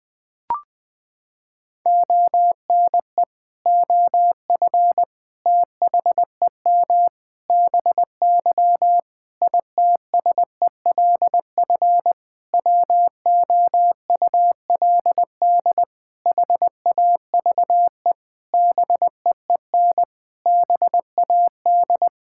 Morse code, heard as a keyed tone, about 21 wpm, 700 Hz. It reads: ONE OF THEM BY ITSELF WOULD HAVE BEEN BAD
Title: Sentences from Top 1000 Words - Review Part 5 - 20wpm